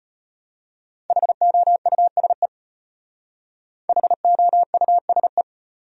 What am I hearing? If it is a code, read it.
Morse code, heard as a keyed tone, about 31 wpm, 700 Hz. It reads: HOUSE HOUSE